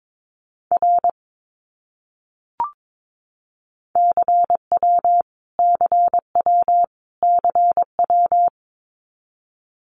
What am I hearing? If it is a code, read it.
Morse code, heard as a keyed tone, about 22 wpm, 700 Hz. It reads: R CW CW CW
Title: Abbreviations Repeated 3x - 22wpm